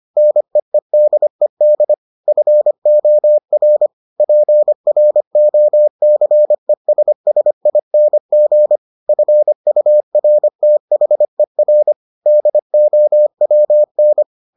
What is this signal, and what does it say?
Morse code, heard as a keyed tone, about 25 wpm, 600 Hz. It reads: NEEDED FOR PROCESSING FURTHER DOWN